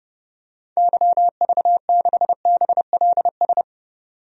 Morse code, heard as a keyed tone, about 30 wpm, 700 Hz. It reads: YV6BLS
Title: DXCC-100 Call Signs - 6 Characters - 30wpm